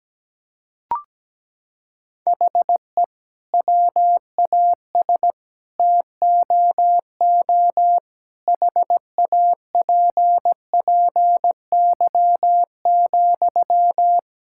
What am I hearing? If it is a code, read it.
Morse code, heard as a keyed tone, about 17 wpm, 700 Hz. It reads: HE WAS TOO HAPPY,
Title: Sentences from Top 1000 Words - Review Part 6 - 17wpm